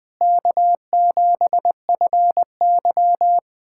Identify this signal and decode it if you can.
Morse code, heard as a keyed tone, about 20 wpm, 700 Hz. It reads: K7FY
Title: USA Callsign DE Callsign - 4 Characters - 20wpm